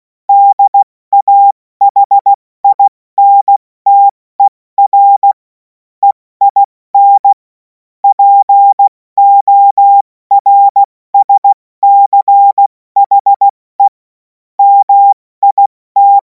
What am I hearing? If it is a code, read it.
Morse code, heard as a keyed tone, about 16 wpm, 800 Hz. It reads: DAHINTER EIN PORSCHE MIT
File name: Morsekurs / Witze / Witz 05. Mai 2026 70 F80